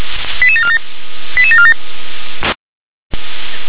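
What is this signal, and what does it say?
Five-tone selective call (ZVEI), heard as quick runs of five tones, tones of about 70 ms. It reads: 88747 88747
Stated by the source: ZVEI five-tone selective call received by SDR.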